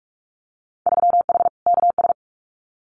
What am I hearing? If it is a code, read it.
Morse code, heard as a keyed tone, about 46 wpm, 700 Hz. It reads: 3H KS
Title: Winter Field Day Exchanges 45wpm